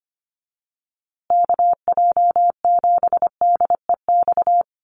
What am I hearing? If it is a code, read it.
Morse code, heard as a keyed tone, about 25 wpm, 700 Hz. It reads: KJ7DEX
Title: USA Call Signs Repeated 3x 25wpm